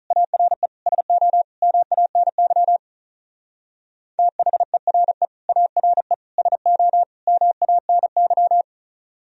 Morse code, heard as a keyed tone, about 42 wpm, 700 Hz. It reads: ARE SO MANY THERE ARE SO MANY
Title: Speed-Racing - Sets of 4 Words 35wpm